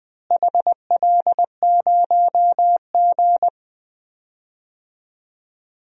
Morse code, heard as a keyed tone, about 21 wpm, 700 Hz. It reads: HL0G
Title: DXCC-100 Call Signs - 4 Characters - Repeated 3x - 20wpm